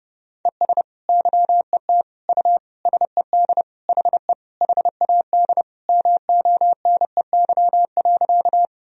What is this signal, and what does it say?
Morse code, heard as a keyed tone, about 30 wpm, 700 Hz. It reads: ES YET U SED HE HAD MONEY.
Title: Sentences from Top 500 Words and Abbreviations 30wpm